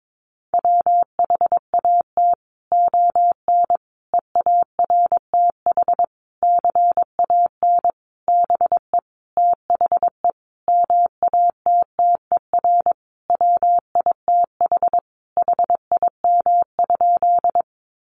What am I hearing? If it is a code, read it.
Morse code, heard as a keyed tone, about 22 wpm, 700 Hz. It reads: WHAT ON EARTH CAN BE THE MATTER WITH HIM?
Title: Sentences from Top 700 Words 22wpm